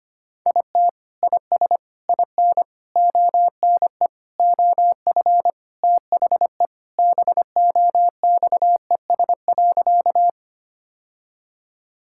Morse code, heard as a keyed tone, about 25 wpm, 700 Hz. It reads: IT IS IN ONE OF THE BOXES.
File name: practice / 25wpm / Sentences with 3-Word Phrases - 25wpm